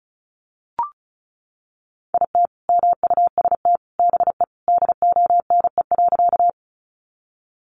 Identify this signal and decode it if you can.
Morse code, heard as a keyed tone, about 34 wpm, 700 Hz. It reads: IT MUST BE DONE.
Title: Sentences from Top 500 Words 35wpm